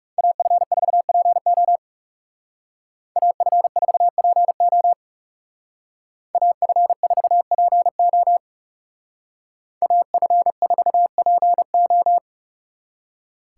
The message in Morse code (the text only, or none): AF4PO AF4PO AF4PO AF4PO